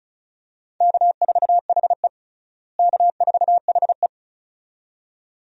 Morse code, heard as a keyed tone, about 36 wpm, 700 Hz. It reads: K4HE K4HE
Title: USA Call Signs - 4 Characters - Repeated 2x - 35wpm